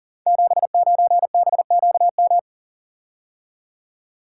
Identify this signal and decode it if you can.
Morse code, heard as a keyed tone, about 39 wpm, 690 Hz. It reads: 79BQM
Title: Code Groups - 5 Letter:Numbers - 40wpm